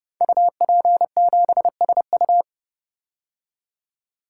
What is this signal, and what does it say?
Morse code, heard as a keyed tone, about 29 wpm, 700 Hz. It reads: UP7SU
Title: Code Groups - 5 Letter:Numbers - 30wpm